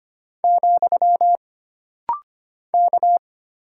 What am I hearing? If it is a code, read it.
Morse code, heard as a keyed tone, about 25 wpm, 700 Hz. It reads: , K
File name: Single Character (Letter:Number:Punctuation:Prosign) - Rapid-Fire - 25wpm